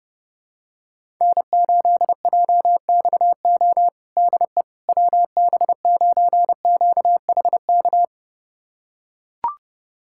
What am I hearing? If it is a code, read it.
Morse code, heard as a keyed tone, about 30 wpm, 700 Hz. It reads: N8JXO DE WB9QHK
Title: USA Callsign DE Callsign 30wpm